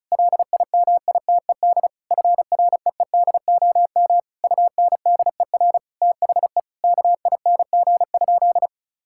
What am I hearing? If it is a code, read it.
Morse code, heard as a keyed tone, about 36 wpm, 700 Hz. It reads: LIMITED FREEDOM UNDER THE KING?
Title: Sentences from Top 1000 Words - Review Part 6 - 35wpm